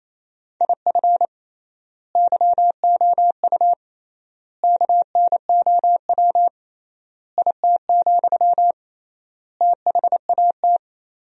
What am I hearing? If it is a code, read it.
Morse code, heard as a keyed tone, about 28 wpm, 700 Hz. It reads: IF YOU KNOW IT, THAT
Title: 3x Word-Spacing - Sentences from Top 400 Words - 28wpm